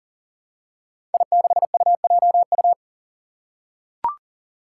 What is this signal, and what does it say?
Morse code, heard as a keyed tone, about 40 wpm, 700 Hz. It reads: I6UJU
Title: Five Challenging Letter-Number Combinations 40wpm